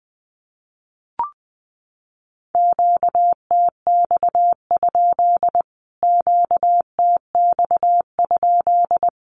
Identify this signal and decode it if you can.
Morse code, heard as a keyed tone, about 20 wpm, 700 Hz. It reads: QTX? QTX?
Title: Q-Codes Repeated 3x 20wpm